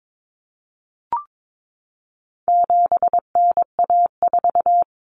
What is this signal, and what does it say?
Morse code, heard as a keyed tone, about 22 wpm, 700 Hz. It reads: 7NA4
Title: Common Cut Numbers - 4-Digit 22wpm